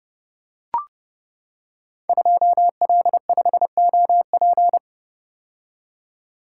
Morse code, heard as a keyed tone, about 30 wpm, 700 Hz. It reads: 2L5OP
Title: Code Groups - 5 Letter:Numbers - 30wpm